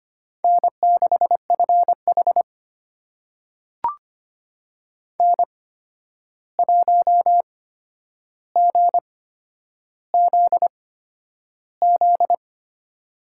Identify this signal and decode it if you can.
Morse code, heard as a keyed tone, about 25 wpm, spaced as for 10 wpm, 700 Hz. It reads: N6FH N1GZZ
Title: USA Call Signs - 8x Character Spacing - 25wpm